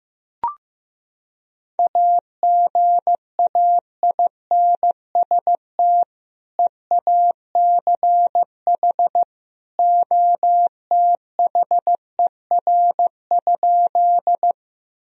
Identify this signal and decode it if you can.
Morse code, heard as a keyed tone, about 15 wpm, 700 Hz. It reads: AGAINST EACH OTHER?
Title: Sentences from Top 1000 Words - Review Part 1 - 15wpm